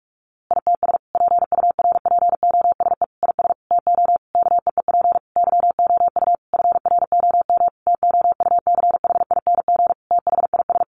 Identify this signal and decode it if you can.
Morse code, heard as a keyed tone, about 44 wpm, 700 Hz. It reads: ITS PURPOSE IS TO KEEP YOU FROM TOUCHING THIS
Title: A Sound Of Thunder 45wpm